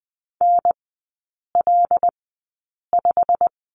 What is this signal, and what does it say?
Morse code, heard as a keyed tone, about 20 wpm, spaced as for 8 wpm, 690 Hz. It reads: NL5
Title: 166 Lesson 27e - Number 7 - Three Characters 20wpm